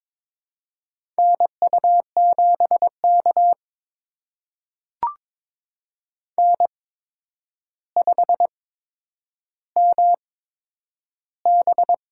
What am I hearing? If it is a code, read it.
Morse code, heard as a keyed tone, about 22 wpm, spaced as for 9 wpm, 700 Hz. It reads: NU7K N5MB
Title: USA Call Signs - 8x Character Spacing - 22wpm